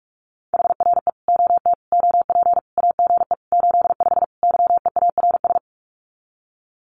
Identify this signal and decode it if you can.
Morse code, heard as a keyed tone, about 44 wpm, 700 Hz. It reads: HRE OT OP AGE 85 YEARS
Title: Common QSO Phrases 45wpm